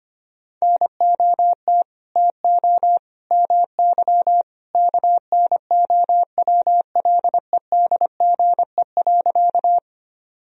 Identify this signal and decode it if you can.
Morse code, heard as a keyed tone, about 25 wpm, 700 Hz. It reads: NOT TO MY KNOWLEDGE.
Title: Sentences from Top 1000 Words - Review Part 3 - 25wpm